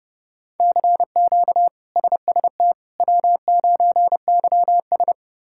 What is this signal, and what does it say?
Morse code, heard as a keyed tone, about 30 wpm, 700 Hz. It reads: CQ SST W9YS
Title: K1USN SST QSO Practice - 30wpm